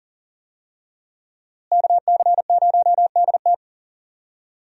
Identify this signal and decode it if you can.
Morse code, heard as a keyed tone, about 40 wpm, 700 Hz. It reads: KC0DT
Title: USA Call Signs - 5 Characters - Repeated 3x - 40wpm